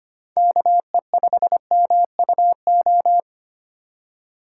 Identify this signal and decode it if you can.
Morse code, heard as a keyed tone, about 25 wpm, 700 Hz. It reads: KE5MUO
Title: USA Call Signs - 3x Character Spacing - 25wpm